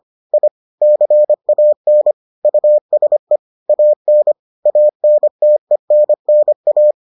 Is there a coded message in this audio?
Yes: Morse code from 0.3 to 6.9 s, heard as a keyed tone, about 25 wpm, 600 Hz. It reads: I CAN USE AN ANTENNA